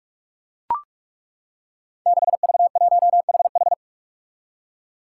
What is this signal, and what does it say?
Morse code, heard as a keyed tone, about 42 wpm, 700 Hz. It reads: 6V1HH
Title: Five Challenging Letter-Number Combinations 45wpm